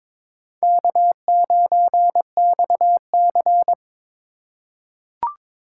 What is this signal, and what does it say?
Morse code, heard as a keyed tone, about 23 wpm, 700 Hz. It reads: K9XC
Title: USA Call Signs - 4 Characters - 2x Character Spacing - 22wpm